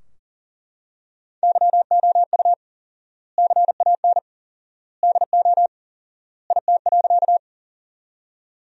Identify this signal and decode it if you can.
Morse code, heard as a keyed tone, about 40 wpm, 700 Hz. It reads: YOU CAN DO IT.
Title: 4x Word-Spacing - Sentences from Top 100 Words - 40wpm